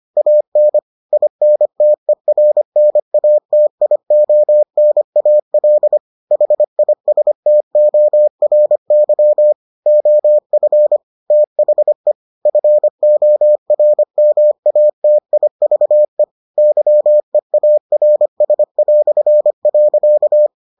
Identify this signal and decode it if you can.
Morse code, heard as a keyed tone, about 25 wpm, 600 Hz. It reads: AN INTERNATIONAL HISTORY OF THE FORMATIVE YEARS".